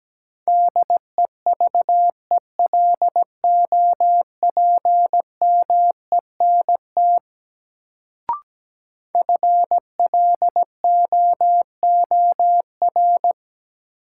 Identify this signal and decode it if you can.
Morse code, heard as a keyed tone, about 17 wpm, 700 Hz. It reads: DEVELOPMENT FLOOR
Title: Sets of 3 Words - New Word Drill - 17wpm